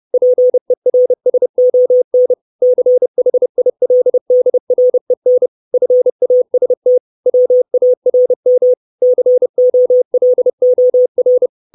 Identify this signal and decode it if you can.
Morse code, heard as a keyed tone, about 30 wpm, 500 Hz. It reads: PERSON CHILDREN FAST WARM COLOR
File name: 632 Words 30WPM